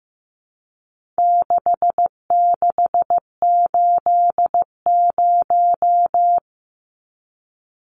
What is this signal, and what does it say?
Morse code, heard as a keyed tone, about 15 wpm, 700 Hz. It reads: 6680
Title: Numbers - 4-Digit 15wpm